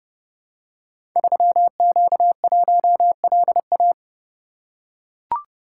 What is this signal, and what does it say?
Morse code, heard as a keyed tone, about 30 wpm, 700 Hz. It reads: 3Q1LA
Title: Code Groups - 5 Letter:Numbers - 30wpm